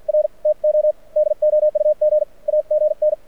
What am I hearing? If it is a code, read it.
Morse code, heard as a keyed tone, about 53 wpm, spaced as for 46 wpm, 590 Hz. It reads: R TO DOUG AGN
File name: archived / CRSnet